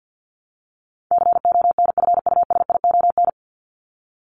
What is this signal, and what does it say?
Morse code, heard as a keyed tone, about 50 wpm, 700 Hz. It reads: CONFUSION
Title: Top 1501-1750 Words 50wpm